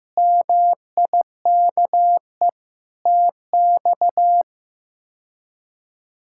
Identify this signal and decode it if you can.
Morse code, heard as a keyed tone, about 16 wpm, 700 Hz. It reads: MIKE TX
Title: K1USN SST QSO Practice - 15wpm